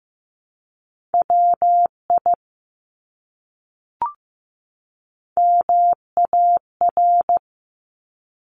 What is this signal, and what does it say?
Morse code, heard as a keyed tone, about 15 wpm, 700 Hz. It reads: WI MAR
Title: ARRL Sections 15wpm